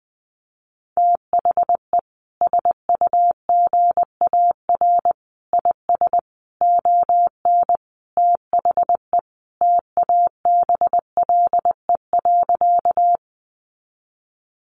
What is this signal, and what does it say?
Morse code, heard as a keyed tone, about 20 wpm, 700 Hz. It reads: THE SUGAR IS ON THE TABLE.
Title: Sentences from Top 1200 Words 20wpm